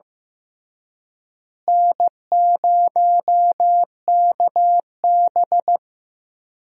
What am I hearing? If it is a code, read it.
Morse code, heard as a keyed tone, about 15 wpm, 700 Hz. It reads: N0KB
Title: USA Call Signs - 4 Characters - Repeated 3x - 15wpm